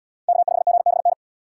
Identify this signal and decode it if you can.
Morse code, heard as a keyed tone, about 68 wpm, 700 Hz. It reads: K5GBN